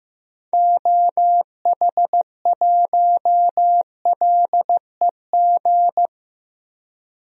Morse code, heard as a keyed tone, about 15 wpm, 700 Hz. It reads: OH1LEG
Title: DXCC-100 Call Signs - 6 Characters - 15wpm